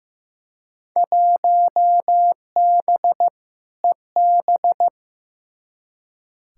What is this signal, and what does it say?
Morse code, heard as a keyed tone, about 16 wpm, 700 Hz. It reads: 1B EB
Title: Field Day Exchanges - 15wpm